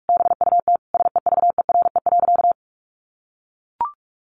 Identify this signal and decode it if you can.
Morse code, heard as a keyed tone, about 44 wpm, 700 Hz. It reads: BUT SEVERE.
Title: Sentences from Top 1400 Words 45wpm